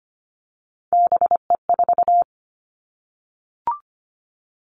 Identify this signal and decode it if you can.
Morse code, heard as a keyed tone, about 25 wpm, 700 Hz. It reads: BE4
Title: Three Challenging Letter-Number Combinations 25wpm